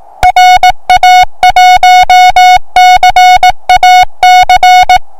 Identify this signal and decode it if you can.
Morse code, heard as a keyed tone, about 18 wpm, 730 Hz. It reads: RA1CAC
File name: attachments / 1cq noagc 2cq agcin 3cq agcout